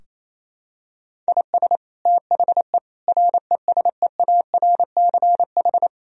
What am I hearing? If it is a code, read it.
Morse code, heard as a keyed tone, about 28 wpm, 700 Hz. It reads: IS THE RESEARCH
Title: Sentences from Top 1000 Words - Review Part 4 - 28wpm